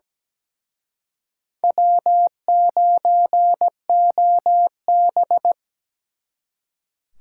Morse code, heard as a keyed tone, about 20 wpm, 700 Hz. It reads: W9OB
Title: Speed Racing - USA 1x2 Call Signs 17wpm